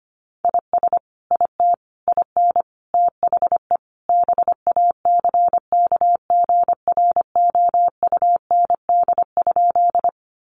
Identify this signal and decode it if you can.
Morse code, heard as a keyed tone, about 25 wpm, 700 Hz. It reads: IS IT IN THE BACKGROUND?
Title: Sentences from Top 1300 Words 25wpm